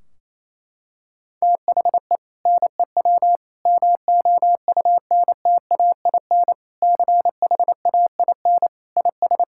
Morse code, heard as a keyed tone, about 28 wpm, 700 Hz. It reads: THE NEW MOUNTAIN CHAIN IS
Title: Sentences from Top 1600 Words 28wpm